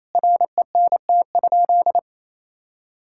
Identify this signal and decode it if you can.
Morse code, heard as a keyed tone, about 28 wpm, 700 Hz. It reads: RENT?
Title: Sentences from Top 1000 Words - Review Part 5 - 28wpm